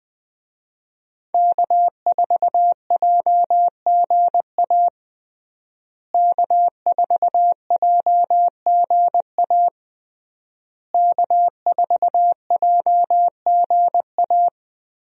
Morse code, heard as a keyed tone, about 20 wpm, 700 Hz. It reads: K4JGA K4JGA K4JGA
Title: USA Call Signs - 5 Characters - Repeated 3x - 20wpm